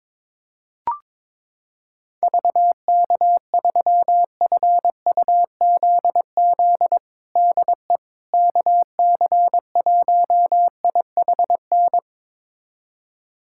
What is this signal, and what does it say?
Morse code, heard as a keyed tone, about 21 wpm, 700 Hz. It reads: VK3FUZZ DE KC1IHN
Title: USA Callsign DE Callsign 22wpm